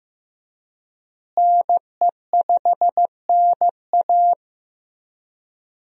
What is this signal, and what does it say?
Morse code, heard as a keyed tone, about 15 wpm, 700 Hz. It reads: NE5NA
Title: USA Call Signs - 5 Characters - Repeated 2x - 15wpm